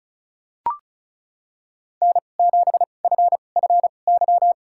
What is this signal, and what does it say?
Morse code, heard as a keyed tone, about 34 wpm, spaced as for 25 wpm, 700 Hz. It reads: N7FFY
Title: USA Call Signs - 5 Characters - 2x Character Spacing - 35wpm